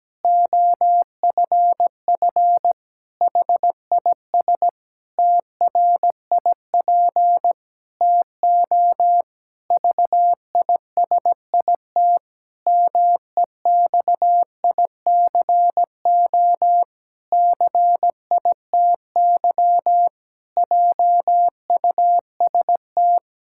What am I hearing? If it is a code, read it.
Morse code, heard as a keyed tone, about 17 wpm, 700 Hz. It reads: OFF HIS TRIP TO VISIT MEXICO CITY JUST
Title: Pangram Challenge v2 - 17wpm